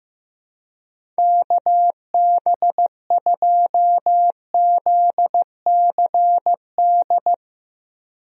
Morse code, heard as a keyed tone, about 15 wpm, 700 Hz. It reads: KB2ZCD
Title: USA Call Signs - 6 Characters - Repeated 2x - 15wpm